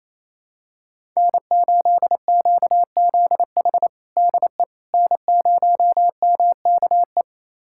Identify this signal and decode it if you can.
Morse code, heard as a keyed tone, about 27 wpm, 700 Hz. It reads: N8QZH DE N0MKE
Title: USA Callsign DE Callsign - 5 Characters - 28wpm